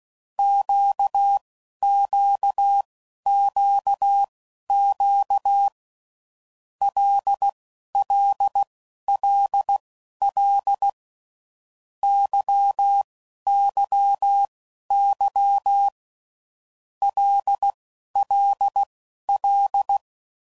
Morse code, heard as a keyed tone, about 16 wpm, spaced as for 11 wpm, 800 Hz. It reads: QQQQ LLLL YYY LLL
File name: Groups / Basic Letter Groups / LFCYQ